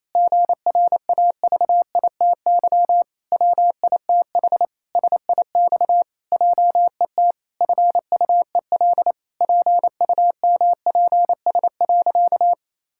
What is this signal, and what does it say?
Morse code, heard as a keyed tone, about 28 wpm, 700 Hz. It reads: GRAVITY WITH SIX JET FUEL PUMPS.
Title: Pangram Challenge v2 - 28wpm